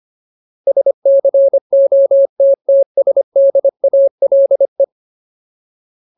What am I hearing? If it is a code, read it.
Morse code, heard as a keyed tone, about 25 wpm, 550 Hz. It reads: SCOTTSDALE
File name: Ultimate Word List / 25WPM Ultimate